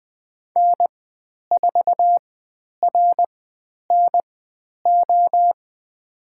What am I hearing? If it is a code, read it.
Morse code, heard as a keyed tone, about 20 wpm, spaced as for 10 wpm, 700 Hz. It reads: N4RNO
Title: 102 Lesson 17g - Letter u - Call Signs 20wpm